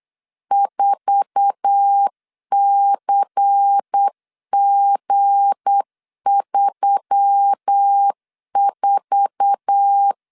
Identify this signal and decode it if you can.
Morse code, heard as a keyed tone, about 8 wpm, 800 Hz. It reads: 4CG34
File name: LEZIONI / 8 YJ45 / 8 YJ4 8wpm 3246